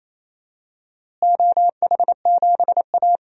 Morse code, heard as a keyed tone, about 27 wpm, 700 Hz. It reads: OH7A
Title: DXCC-100 Call Signs - 4 Characters - Repeated 3x - 28wpm